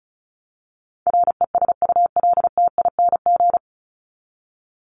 Morse code, heard as a keyed tone, about 34 wpm, 700 Hz. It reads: RESULTING
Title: ING Suffix - Top 100 Words with Base Words 35wpm